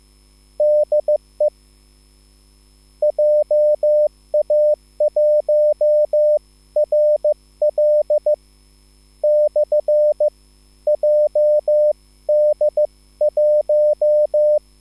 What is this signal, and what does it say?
Morse code, heard as a keyed tone, about 15 wpm, spaced as for 12 wpm, 600 Hz. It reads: DE JA1RL/JD1
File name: QSL / CW lesson